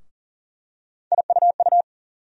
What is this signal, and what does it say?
Morse code, heard as a keyed tone, about 37 wpm, 700 Hz. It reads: IUU
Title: USA Call Sign Postfixes - 3 Letters 40wpm